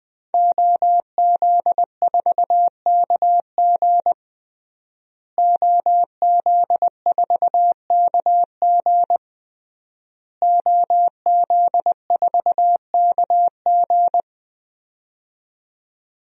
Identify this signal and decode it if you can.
Morse code, heard as a keyed tone, about 20 wpm, 700 Hz. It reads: OZ4KG OZ4KG OZ4KG
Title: DXCC-100 Call Signs - 5 Characters - Repeated 3x - 20wpm